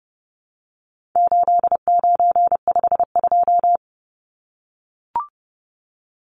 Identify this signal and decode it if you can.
Morse code, heard as a keyed tone, about 30 wpm, 700 Hz. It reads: 8952
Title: Numbers - 4-Digit 30wpm